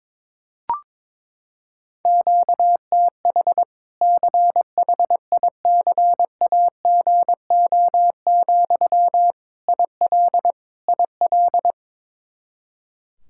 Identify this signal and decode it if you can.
Morse code, heard as a keyed tone, about 22 wpm, 700 Hz. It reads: QTH CHICAGO, IL IL
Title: Practice QSO - Exchange 1 - Beginning - 22wpm